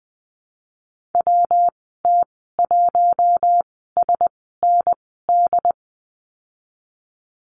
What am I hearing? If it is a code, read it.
Morse code, heard as a keyed tone, about 20 wpm, spaced as for 15 wpm, 700 Hz. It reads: WT1SND
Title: 167 Lesson 27g - Number 7 - Call Signs 20wpm